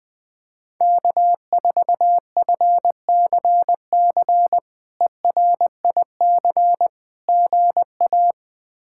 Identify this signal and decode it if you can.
Morse code, heard as a keyed tone, about 21 wpm, 700 Hz. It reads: K4FCC ERIC GA
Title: K1USN SST QSO Practice - 20wpm